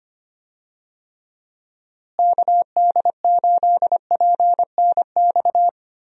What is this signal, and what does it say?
Morse code, heard as a keyed tone, about 25 wpm, 700 Hz. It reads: KD8PNX
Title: USA Call Signs - 6 Characters - Repeated 3x - 25wpm